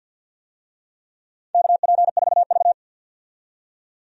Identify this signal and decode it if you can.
Morse code, heard as a keyed tone, about 52 wpm, 700 Hz. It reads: KP4V
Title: DXCC-100 Call Signs - 4 Characters - 50wpm